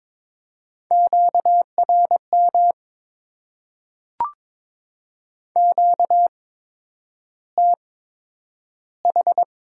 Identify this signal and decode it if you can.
Morse code, heard as a keyed tone, about 22 wpm, spaced as for 9 wpm, 700 Hz. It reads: QRM QTH
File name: Top 90 QSO Elements - 8x Character Spacing - 22wpm